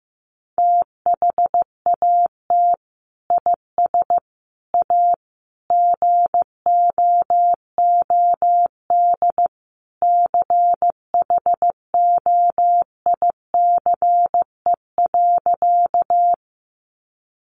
Morse code, heard as a keyed tone, about 15 wpm, 700 Hz. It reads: THAT IS A GOOD CHOICE.
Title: Sentences from Top 900 Words 15wpm